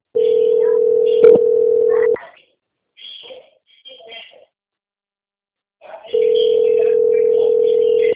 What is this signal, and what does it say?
Ringback tone (North America), 440 + 480 Hz, with speech.
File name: out- - - - - .23055